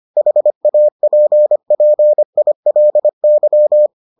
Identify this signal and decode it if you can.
Morse code, heard as a keyed tone, about 25 wpm, 600 Hz. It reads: HAPPILY